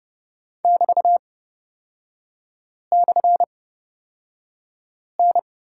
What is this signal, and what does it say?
Morse code, heard as a keyed tone, about 30 wpm, spaced as for 5 wpm, 700 Hz. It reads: <BT>/N
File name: Lesson 36e - BT - Three Characters 30wpm